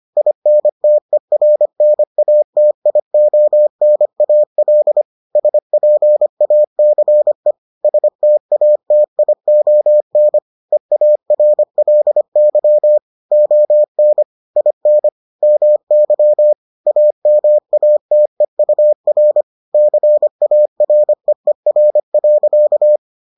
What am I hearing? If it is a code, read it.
Morse code, heard as a keyed tone, about 25 wpm, 600 Hz. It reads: INTERNATIONAL SPACE STATION EARLY ON IN MY AMATEUR CAREER.